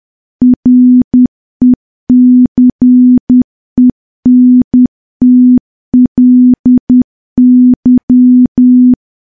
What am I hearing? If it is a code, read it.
Morse code, heard as a keyed tone, about 10 wpm, 250 Hz. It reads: RECENTLY
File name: morse / morse-training-10WPM